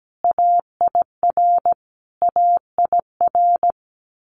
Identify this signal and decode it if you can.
Morse code, heard as a keyed tone, about 17 wpm, 700 Hz. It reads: AIR AIR
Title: Multi-letter Phonemes Repeated 3x 17wpm